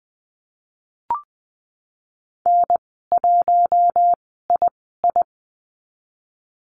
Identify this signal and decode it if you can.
Morse code, heard as a keyed tone, about 20 wpm, spaced as for 15 wpm, 700 Hz. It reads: N1II
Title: 055 Lesson 10g - Letter r - Call Signs 20wpm